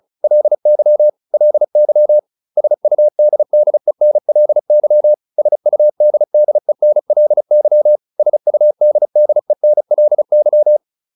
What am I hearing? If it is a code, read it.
Morse code, heard as a keyed tone, about 35 wpm, 600 Hz. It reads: LY LY SUDDENLY SUDDENLY SUDDENLY